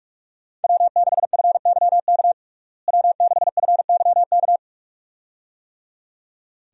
Morse code, heard as a keyed tone, about 47 wpm, 700 Hz. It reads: W6FYK W6FYK
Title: USA Call Signs - 5 Characters - Repeated 2x - 45wpm